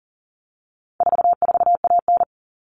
Morse code, heard as a keyed tone, about 40 wpm, 700 Hz. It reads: 44AN
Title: All Cut Numbers - 4-Digit 40wpm